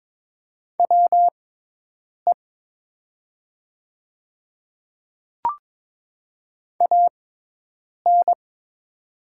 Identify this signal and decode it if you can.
Morse code, heard as a keyed tone, about 22 wpm, spaced as for 8 wpm, 700 Hz. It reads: WE AN
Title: 6x Character Spacing - Top 100 Words - 22wpm